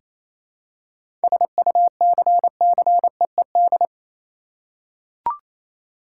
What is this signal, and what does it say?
Morse code, heard as a keyed tone, about 27 wpm, 700 Hz. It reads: SUCCEED
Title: Seven Letter Words - 28wpm